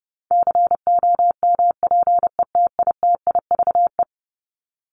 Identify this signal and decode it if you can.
Morse code, heard as a keyed tone, about 30 wpm, 700 Hz. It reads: COMPETITIVE
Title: Top 1001-1250 Words 30wpm